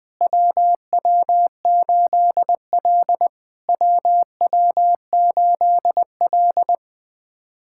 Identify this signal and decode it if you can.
Morse code, heard as a keyed tone, about 20 wpm, 700 Hz. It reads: WW8L WW8L
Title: POTA QSO Practice - 20wpm